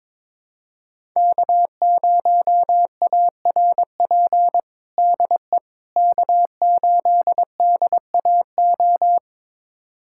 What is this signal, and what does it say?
Morse code, heard as a keyed tone, about 23 wpm, 700 Hz. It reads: K0ARP DE K8DAO
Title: USA Callsign DE Callsign - 5 Characters - 22wpm